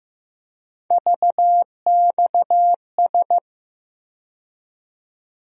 Morse code, heard as a keyed tone, about 15 wpm, 700 Hz. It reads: VXS
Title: USA Call Sign Postfixes - 3 Letters 15wpm